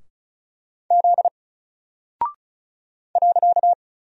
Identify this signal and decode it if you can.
Morse code, heard as a keyed tone, about 34 wpm, 700 Hz. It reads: Z .
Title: Single Character (Letter:Number:Punctuation:Prosign) 35wpm